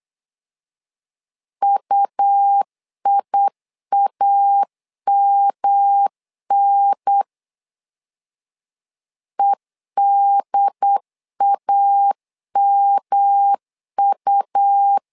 Morse code, heard as a keyed tone, about 8 wpm, 800 Hz. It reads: UIAMN EDAMU
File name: LEZIONI / 3 MWRO / 3 M 8wpm 2300